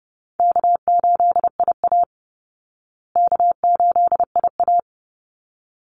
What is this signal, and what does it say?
Morse code, heard as a keyed tone, about 30 wpm, 700 Hz. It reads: K8IA K8IA
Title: Easy Call Signs Repeated 2x - 30wpm